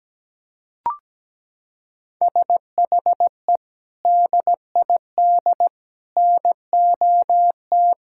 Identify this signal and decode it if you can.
Morse code, heard as a keyed tone, about 17 wpm, 700 Hz. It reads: SHE DID NOT
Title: Sentences from Top 1000 Words - Review Part 6 - 17wpm